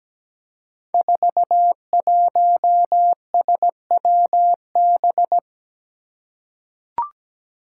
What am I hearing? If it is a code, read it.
Morse code, heard as a keyed tone, about 17 wpm, 700 Hz. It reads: 41SWB
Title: Five Challenging Letter-Number Combinations 17wpm